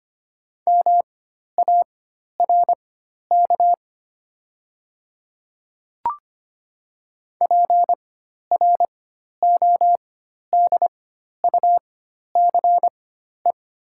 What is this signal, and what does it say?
Morse code, heard as a keyed tone, about 25 wpm, spaced as for 12 wpm, 700 Hz. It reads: MARK PRODUCE
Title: 4x Character Spacing - Top 500 Words - 25wpm